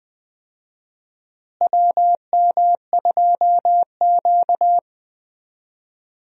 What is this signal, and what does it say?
Morse code, heard as a keyed tone, about 20 wpm, 700 Hz. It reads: WM2Q
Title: USA Call Signs - 4 Characters - Repeated 2x - 20wpm